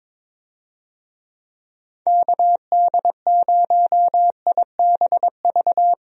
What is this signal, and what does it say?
Morse code, heard as a keyed tone, about 22 wpm, 700 Hz. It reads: KD0IBV
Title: USA Call Signs - 6 Characters - Repeated 2x - 22wpm